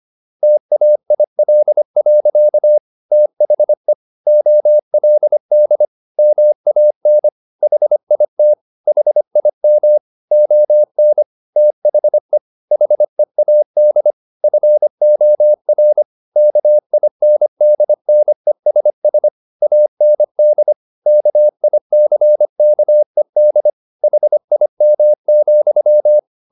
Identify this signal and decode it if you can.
Morse code, heard as a keyed tone, about 25 wpm, 600 Hz. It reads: TAIL. THE OLD MAN HIT HIM ON THE HEAD FOR KINDNESS AND KICKED HIM,
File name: the old man and the sea part 1 25wpm